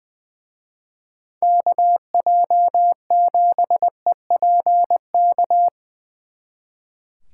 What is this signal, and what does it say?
Morse code, heard as a keyed tone, about 22 wpm, 700 Hz. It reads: KJ7EPK
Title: Speed Racing - USA 2x3 Call Signs 20wpm